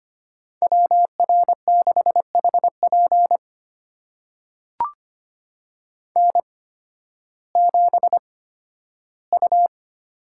Morse code, heard as a keyed tone, about 25 wpm, spaced as for 13 wpm, 700 Hz. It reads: WR6HP N7U
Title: USA Call Signs - 5 Characters - 8x Character Spacing - 25wpm